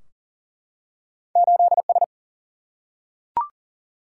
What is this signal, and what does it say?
Morse code, heard as a keyed tone, about 40 wpm, 700 Hz. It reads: 8S
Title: Single-Letter Single-Number 40wpm